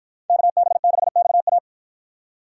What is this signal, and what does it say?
Morse code, heard as a keyed tone, about 51 wpm, 700 Hz. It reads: KB6XA